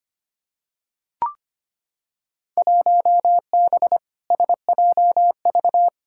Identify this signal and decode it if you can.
Morse code, heard as a keyed tone, about 25 wpm, 700 Hz. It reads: 1B SJV
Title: Field Day Exchanges - 25wpm